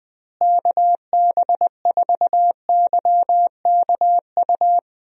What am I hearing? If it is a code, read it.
Morse code, heard as a keyed tone, about 20 wpm, 700 Hz. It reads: KB4YKU
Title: USA Call Signs - 6 Characters - Repeated 2x - 20wpm